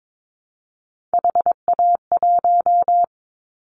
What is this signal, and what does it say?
Morse code, heard as a keyed tone, about 22 wpm, 700 Hz. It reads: HA1
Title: Callsign Prefixes for Top 25 DXCC - 22wpm